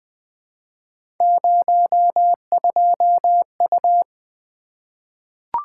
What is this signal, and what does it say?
Morse code, heard as a keyed tone, about 20 wpm, 700 Hz. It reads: 02U
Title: Three Letter-Number Combinations - 20wpm